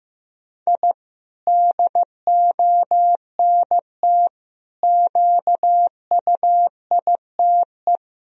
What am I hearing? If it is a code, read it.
Morse code, heard as a keyed tone, about 15 wpm, 700 Hz. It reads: I DONT QUITE
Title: Sentences from Top 1000 Words - Review Part 4 - 15wpm